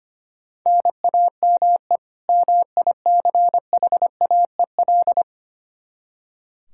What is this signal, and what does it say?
Morse code, heard as a keyed tone, about 25 wpm, 700 Hz. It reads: NAME MICHAEL
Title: Practice QSO - Exchange 1 - Beginning - 25wpm